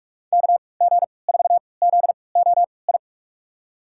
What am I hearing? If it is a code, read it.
Morse code, heard as a keyed tone, about 46 wpm, spaced as for 26 wpm, 700 Hz. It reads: KG4ZOI
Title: USA Call Signs - 6 Characters - 3x Character Spacing - 45wpm